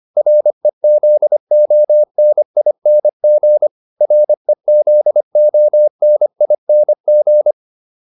Morse code, heard as a keyed tone, about 25 wpm, 600 Hz. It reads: REZONING REZONING